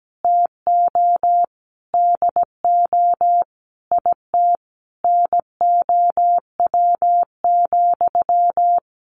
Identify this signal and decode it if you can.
Morse code, heard as a keyed tone, about 17 wpm, 700 Hz. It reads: TO DO IT NOW,
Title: Sentences from Top 600 Words 17wpm